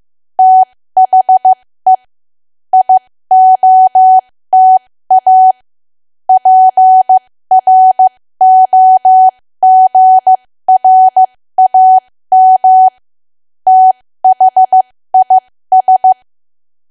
Morse code, heard as a keyed tone, about 15 wpm, spaced as for 13 wpm, 750 Hz. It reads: THE IOTA PROGRAM THIS
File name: cwa / W1AW-Level2 / 13WPM